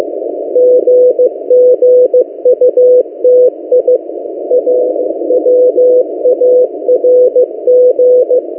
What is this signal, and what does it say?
Morse code, heard as a keyed tone, about 16 wpm, 500 Hz. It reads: GGUTI AWARG
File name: M08a- -1355utc-8096khz